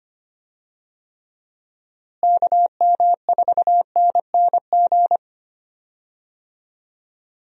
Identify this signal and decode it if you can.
Morse code, heard as a keyed tone, about 25 wpm, 700 Hz. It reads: KM4NNG
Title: USA Call Signs - 6 Characters - Repeated 2x - 25wpm